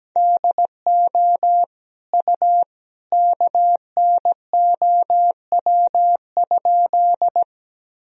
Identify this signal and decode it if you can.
Morse code, heard as a keyed tone, about 17 wpm, 700 Hz. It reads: DO U KNOW?
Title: Sentences from Top 100 Words and Abbreviations 17wpm